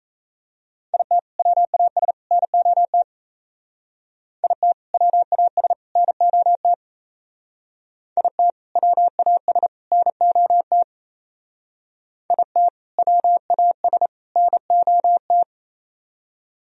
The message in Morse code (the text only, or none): IT WAS NOT IT WAS NOT IT WAS NOT IT WAS NOT